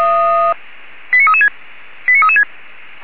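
Five-tone selective call (ZVEI), heard as quick runs of five tones, tones of about 70 ms. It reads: 89336 89336
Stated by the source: Five-tone ZVEI selective call received by SDR, from the Orlamünde fire brigade (feuerwehr-orlamuende) server.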